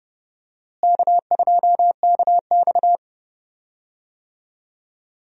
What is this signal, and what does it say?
Morse code, heard as a keyed tone, about 30 wpm, 700 Hz. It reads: K2KX
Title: USA Call Signs - 4 Characters - Repeated 3x - 30wpm